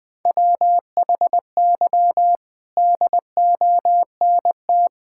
Morse code, heard as a keyed tone, about 19 wpm, 700 Hz. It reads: WHY DONT
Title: Sentences with 5-Word Phrases - 20wpm